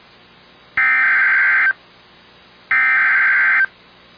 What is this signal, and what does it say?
Emergency Alert System (SAME) tones (header bursts), heard as harsh buzzing bursts. It is heard from 0.8 to 3.6 s.